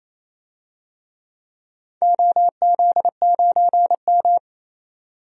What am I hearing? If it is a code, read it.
Morse code, heard as a keyed tone, about 28 wpm, 700 Hz. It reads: OZ9M
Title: DXCC-100 Call Signs - 4 Characters - 28wpm